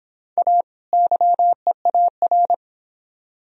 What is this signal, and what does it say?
Morse code, heard as a keyed tone, about 25 wpm, 700 Hz. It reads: A YEAR